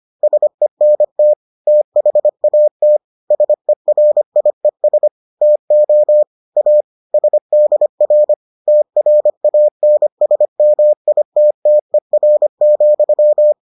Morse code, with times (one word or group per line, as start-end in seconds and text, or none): 0.2-1.3 SENT
1.7-3.0 THAT
3.3-5.1 SERIES
5.4-6.2 TO
6.6-6.8 A
7.1-8.3 SDR
8.7-13.5 TRANSMITTER,